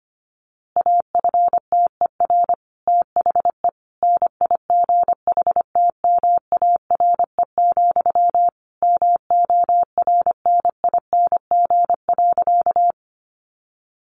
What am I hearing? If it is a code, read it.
Morse code, heard as a keyed tone, about 25 wpm, 700 Hz. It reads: AFTER THE NIGHTMARE, MORNING.
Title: A Sound Of Thunder 25wpm